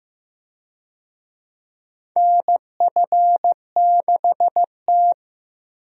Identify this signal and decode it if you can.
Morse code, heard as a keyed tone, about 14 wpm, 700 Hz. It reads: NF6T
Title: USA Call Signs - 4 Characters - Repeated 2x - 15wpm